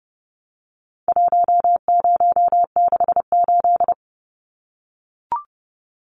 Morse code, heard as a keyed tone, about 30 wpm, 700 Hz. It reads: 1068
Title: Numbers - 4-Digit 30wpm